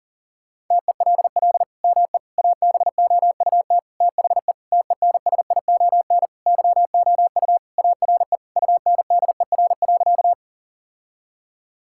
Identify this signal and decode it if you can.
Morse code, heard as a keyed tone, about 40 wpm, 700 Hz. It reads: TELL ME ABOUT THE TENSION YOU ARE UNDER.
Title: Sentences from Top 1600 Words 40wpm